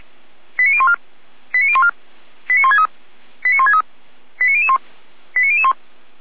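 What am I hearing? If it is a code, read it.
Five-tone selective call (ZVEI), heard as quick runs of five tones, tones of about 70 ms. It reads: 89014 89014 89163 89163 89001 89001
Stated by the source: Five-tone ZVEI selective call received with an SDR.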